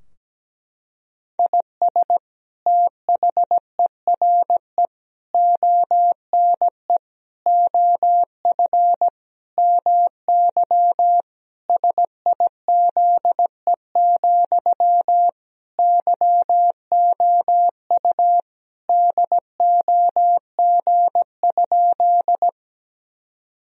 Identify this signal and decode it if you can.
Morse code, heard as a keyed tone, about 17 wpm, 700 Hz. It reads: IS THERE ONE OF MY SIZE, YOU DOG?
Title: Sentences from Top 1000 Words - Review Part 2 - 17wpm